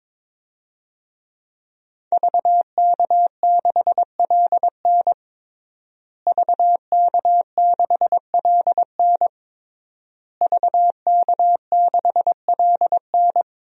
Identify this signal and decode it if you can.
Morse code, heard as a keyed tone, about 22 wpm, 700 Hz. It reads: VK6LN VK6LN VK6LN
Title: DXCC-100 Call Signs - 5 Characters - Repeated 3x - 22wpm